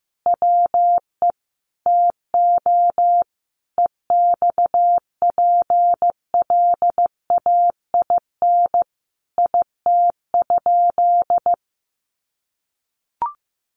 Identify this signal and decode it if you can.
Morse code, heard as a keyed tone, about 15 wpm, 700 Hz. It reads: WE TO EXPLAIN IT?
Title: Sentences from Top 1100 Words 15wpm